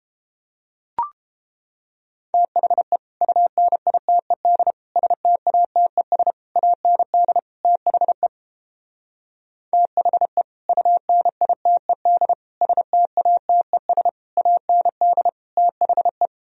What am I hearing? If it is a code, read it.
Morse code, heard as a keyed tone, about 29 wpm, 700 Hz. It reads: THE UNITED STATES AND THE THE UNITED STATES AND THE